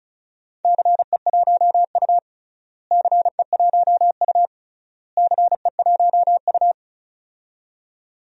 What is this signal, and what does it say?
Morse code, heard as a keyed tone, about 34 wpm, 700 Hz. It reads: CE1U CE1U CE1U
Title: DXCC-100 Call Signs - 4 Characters - Repeated 3x - 35wpm